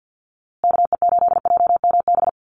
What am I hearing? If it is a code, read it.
Morse code, heard as a keyed tone, about 50 wpm, 700 Hz. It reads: KE8JMB
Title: USA Call Signs Repeated 3x 50wpm